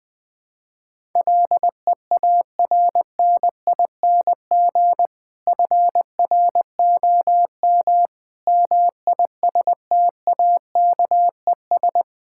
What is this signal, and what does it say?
Morse code, heard as a keyed tone, about 21 wpm, 700 Hz. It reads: LEARNING FROM MISTAKES
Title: Difficult Sentences with Hints - 20wpm